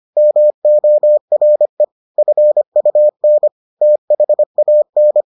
Morse code, heard as a keyed tone, about 25 wpm, 600 Hz. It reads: MORE FUN THAN